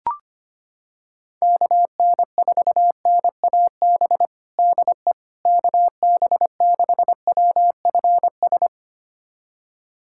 Morse code, heard as a keyed tone, about 25 wpm, 700 Hz. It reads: KN4NAB DE KB6WFS
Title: USA Callsign DE Callsign - 6 Characters - 25wpm